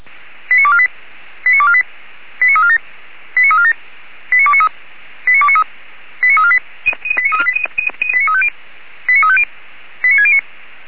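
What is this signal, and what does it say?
Five-tone selective call (ZVEI), heard as quick runs of five tones, tones of about 70 ms. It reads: 89248 89248 89347 89347 89293 89293 89348 89350 89609
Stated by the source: ZVEI five-tone selective call 89406 received on an SDR.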